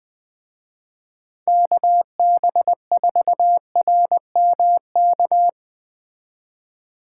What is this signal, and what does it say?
Morse code, heard as a keyed tone, about 20 wpm, 700 Hz. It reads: KB4RMK
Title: Speed Racing - USA 2x3 Call Signs 17wpm